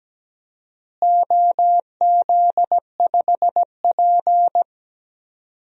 Morse code, heard as a keyed tone, about 17 wpm, 700 Hz. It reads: OZ5P
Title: DXCC-100 Call Signs - 4 Characters - Repeated 2x - 17wpm